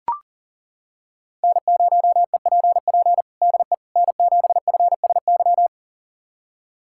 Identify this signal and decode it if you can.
Morse code, heard as a keyed tone, about 40 wpm, 700 Hz. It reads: N0EPP DE N7FSY
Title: USA Callsign DE Callsign - 5 Characters - 40wpm